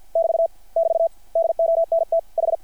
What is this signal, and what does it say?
Morse code, heard as a keyed tone, about 50 wpm, 660 Hz. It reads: <BT> <BT> DONT H
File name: archived / CRSnet